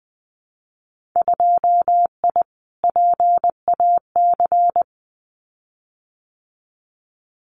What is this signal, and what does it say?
Morse code, heard as a keyed tone, about 20 wpm, 700 Hz. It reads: 2I PAC
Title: Winter Field Day Exchanges Encore 20wpm